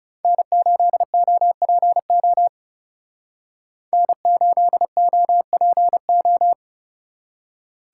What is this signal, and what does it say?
Morse code, heard as a keyed tone, about 36 wpm, 700 Hz. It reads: N8OPO N8OPO
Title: Speed Racing - USA 1x3 Call Signs 30wpm